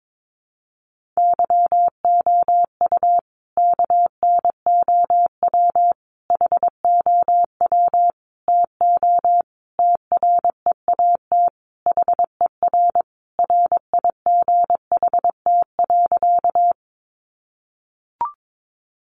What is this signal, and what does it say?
Morse code, heard as a keyed tone, about 22 wpm, 700 Hz. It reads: YOU KNOW HOW TO TREAT HER RIGHT.
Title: Sentences from Top 1300 Words 22wpm